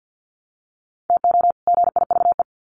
Morse code, heard as a keyed tone, about 50 wpm, 700 Hz. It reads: TO GIVE
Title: Sets of 2 Words 50wpm